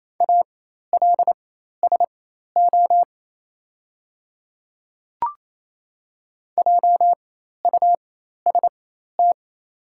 Morse code, heard as a keyed tone, about 28 wpm, spaced as for 13 wpm, 700 Hz. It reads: ALSO JUST
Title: 4x Character Spacing - Top 200 Words - 28wpm